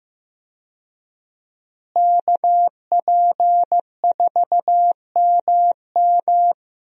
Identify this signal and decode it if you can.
Morse code, heard as a keyed tone, about 15 wpm, 700 Hz. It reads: KP4MM
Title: USA Call Signs - 5 Characters - Repeated 2x - 15wpm